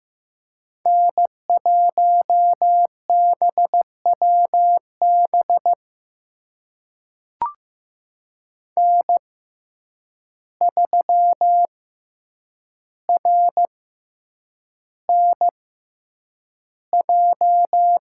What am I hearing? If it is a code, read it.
Morse code, heard as a keyed tone, about 15 wpm, spaced as for 8 wpm, 700 Hz. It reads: N1BWB N3RNJ
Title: USA Call Signs - 5 Characters - 6x Character Spacing - 15wpm